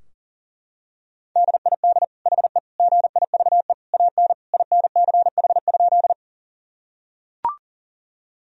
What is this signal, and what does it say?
Morse code, heard as a keyed tone, about 37 wpm, 700 Hz. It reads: DID HE GIVE AN INCH?
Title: Sentences from Top 1000 Words - Review Part 3 - 40wpm